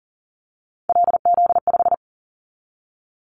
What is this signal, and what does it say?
Morse code, heard as a keyed tone, about 40 wpm, 700 Hz. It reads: LZ5
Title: Callsign Prefixes for Top 100 DXCC - 40wpm